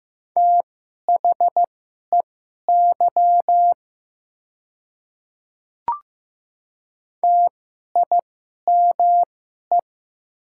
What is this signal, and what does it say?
Morse code, heard as a keyed tone, about 15 wpm, spaced as for 11 wpm, 700 Hz. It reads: THEY TIME
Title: 2x Character Spacing - Top 100 Words - 15wpm